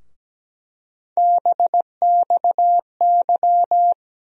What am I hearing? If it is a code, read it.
Morse code, heard as a keyed tone, about 17 wpm, 700 Hz. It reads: BXY
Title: USA Call Sign Postfixes - 3 Letters 17wpm